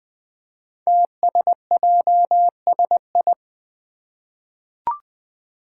Morse code, heard as a keyed tone, about 20 wpm, 700 Hz. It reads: TSJSI
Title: Five Challenging Letter-Number Combinations 20wpm